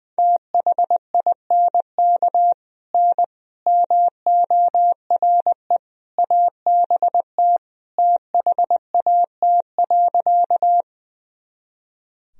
Morse code, heard as a keyed tone, about 20 wpm, 700 Hz. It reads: THINK N MORE ABT THAT.
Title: Sentences from Top 200 Words and Abbreviations 20wpm